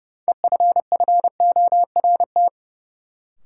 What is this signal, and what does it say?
Morse code, heard as a keyed tone, about 30 wpm, 700 Hz. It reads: EFFORT
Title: Sets of 4 Words - New Word Drill - 30wpm